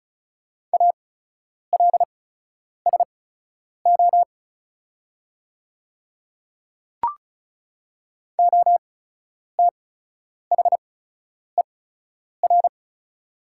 Morse code, heard as a keyed tone, about 35 wpm, spaced as for 10 wpm, 700 Hz. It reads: ALSO OTHER
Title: 8x Character Spacing - Top 300 Words - 35wpm